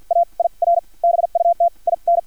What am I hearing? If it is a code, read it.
Morse code, heard as a keyed tone, about 50 wpm, spaced as for 39 wpm, 690 Hz. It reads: AIR BUT IN
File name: archived / CRSnet